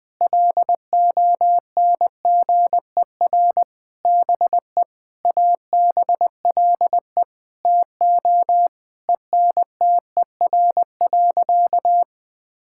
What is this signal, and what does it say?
Morse code, heard as a keyed tone, about 20 wpm, 700 Hz. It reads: LONGER BE ABLE TO ENTER.
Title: Sentences with 6-Word Phrases - Spoken Beginning - 20wpm